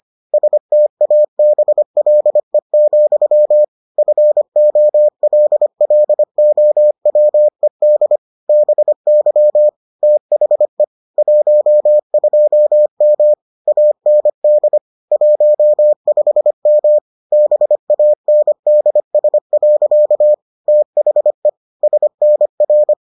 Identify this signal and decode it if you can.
Morse code, heard as a keyed tone, about 25 wpm, 600 Hz. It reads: STABLE, FOLLOWED BY THE 12M AND 15M BANDS. THE SNR